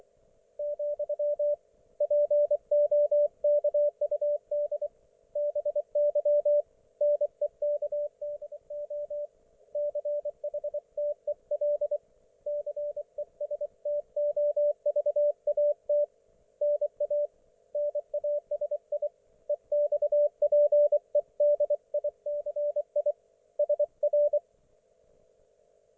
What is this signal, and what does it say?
Morse code, heard as a keyed tone, about 24 wpm, 590 Hz, with radio noise. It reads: , POKUD BY NEKDO CHTEL CESTOVAT NA NASI EXPEDICI SR
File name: Archiv / CUC212 / ok1fhi cul212